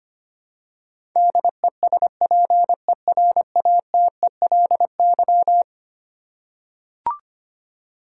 Morse code, heard as a keyed tone, about 25 wpm, 700 Hz. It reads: DESPERATELY
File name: practice / 25wpm / Eleven Letter Words - 25wpm